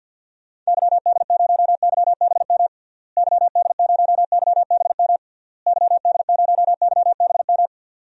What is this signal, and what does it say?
Morse code, heard as a keyed tone, about 52 wpm, 700 Hz. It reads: YD0YBM YD0YBM YD0YBM
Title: DXCC-100 Call Signs - 6 Characters - Repeated 3x - 50wpm